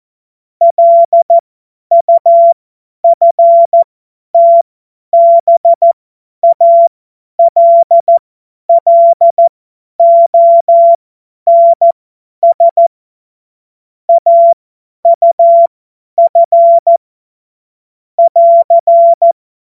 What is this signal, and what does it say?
Morse code, heard as a keyed tone, about 14 wpm, spaced as for 10 wpm, 680 Hz. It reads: LUFTBALLONS AUF <AR>
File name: Morsekurs / Witze / Witz 02. Februar 2021, 50 F70